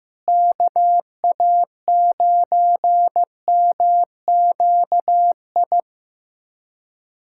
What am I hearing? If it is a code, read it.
Morse code, heard as a keyed tone, about 15 wpm, 700 Hz. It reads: KA9MQI
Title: USA Callsign DE Callsign - 6 Characters - 15wpm